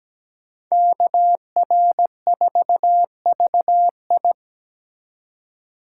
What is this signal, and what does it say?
Morse code, heard as a keyed tone, about 20 wpm, 700 Hz. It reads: KR4VI
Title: Speed Racing - USA Call Signs 17wpm